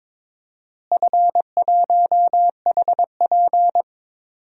Morse code, heard as a keyed tone, about 22 wpm, 700 Hz. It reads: F1HP
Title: DXCC-100 Call Signs - 4 Characters - Repeated 3x - 22wpm